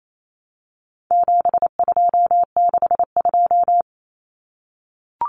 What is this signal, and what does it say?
Morse code, heard as a keyed tone, about 28 wpm, 700 Hz. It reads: 7262
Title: Common Cut Numbers - 4-Digit 28wpm